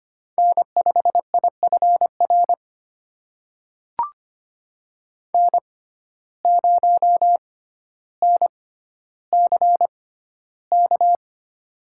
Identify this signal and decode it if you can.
Morse code, heard as a keyed tone, about 25 wpm, spaced as for 13 wpm, 700 Hz. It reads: N5IFR N0NCK
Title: USA Call Signs - 6x Character Spacing - 25wpm